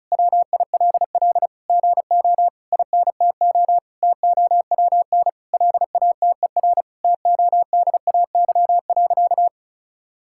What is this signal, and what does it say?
Morse code, heard as a keyed tone, about 34 wpm, 700 Hz. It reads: WILL GO INTO TOWN LATER TODAY.
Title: Sentences from Top 1000 Words - Review Part 6 - 35wpm